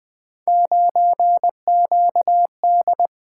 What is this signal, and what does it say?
Morse code, heard as a keyed tone, about 25 wpm, 700 Hz. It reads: 9QD
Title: Speed-Racing - Three Letter-Number Combinations - 20wpm